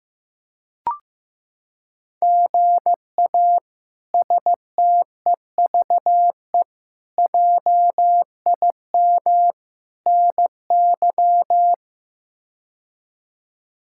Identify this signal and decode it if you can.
Morse code, heard as a keyed tone, about 16 wpm, 700 Hz. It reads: GA STEVE JIM NY
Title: K1USN SST QSO Practice - 15wpm